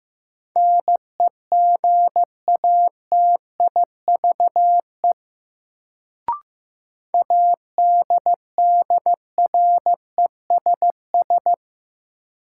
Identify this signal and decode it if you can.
Morse code, heard as a keyed tone, about 15 wpm, 700 Hz. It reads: NEGATIVE ADDRESS
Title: Sentences from Top 1000 Words - New Word Drill - 15wpm